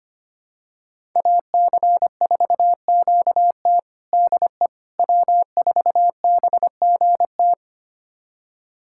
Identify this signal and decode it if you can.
Morse code, heard as a keyed tone, about 25 wpm, 700 Hz. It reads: AC4QT DE W4BGT
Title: USA Callsign DE Callsign - 5 Characters - 25wpm